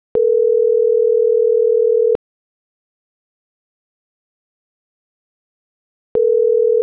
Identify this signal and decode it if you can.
Ringback tone (North America), 440 + 480 Hz.
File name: CENTRAL SAGOT / out isai.hernandez - .87997